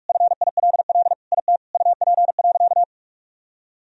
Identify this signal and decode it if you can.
Morse code, heard as a keyed tone, about 45 wpm, 700 Hz. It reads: FILL IT UP.
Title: Sentences from Top 1000 Words - Review Part 3 - 45wpm